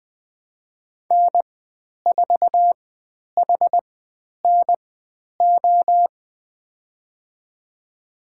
Morse code, heard as a keyed tone, about 20 wpm, spaced as for 10 wpm, 700 Hz. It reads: N4HNO
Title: 109 Lesson 18g - Letter m - Call Signs 20wpm